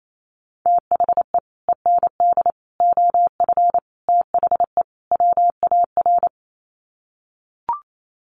Morse code, heard as a keyed tone, about 28 wpm, 700 Hz. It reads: THE END OF THE WAR
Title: Sets of 5 Words 28wpm